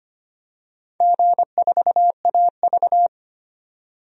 Morse code, heard as a keyed tone, about 25 wpm, 700 Hz. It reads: G4AV
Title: DXCC-100 Call Signs - 4 Characters - 25wpm